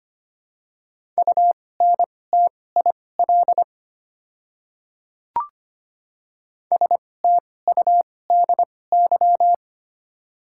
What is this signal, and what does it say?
Morse code, heard as a keyed tone, about 25 wpm, spaced as for 18 wpm, 700 Hz. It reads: UNTIL STUDY
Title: 2x Character Spacing - Top 500 Words - 25wpm